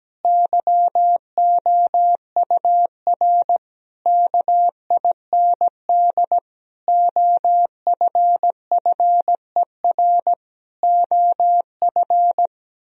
Morse code, heard as a keyed tone, about 17 wpm, 700 Hz. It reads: YOUR KIND OFFER OF
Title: Sentences from Top 1000 Words - Review Part 4 - 17wpm